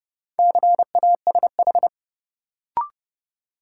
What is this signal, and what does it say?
Morse code, heard as a keyed tone, about 29 wpm, 700 Hz. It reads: CASH
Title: Sentences from Top 700 Words - New Word Drill - 30wpm